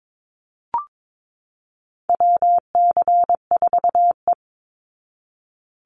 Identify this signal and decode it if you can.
Morse code, heard as a keyed tone, about 22 wpm, 700 Hz. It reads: WC4E
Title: Call Signs Repeated 2x - 22wpm